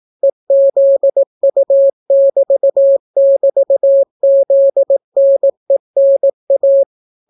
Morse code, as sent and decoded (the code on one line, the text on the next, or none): . --.. ..- -...- -...- --.. -. -. .-
EZU<BT><BT>ZNENA